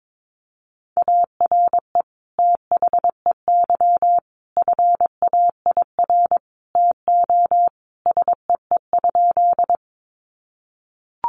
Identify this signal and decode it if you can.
Morse code, heard as a keyed tone, about 22 wpm, 700 Hz. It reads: ARE THEY FAIR TO SEE?
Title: Sentences from Top 1300 Words 22wpm